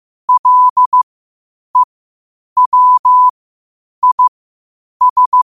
Morse code, heard as a keyed tone, about 15 wpm, spaced as for 8 wpm, 1000 Hz. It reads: LEWIS
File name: qotdfiles / 8wpm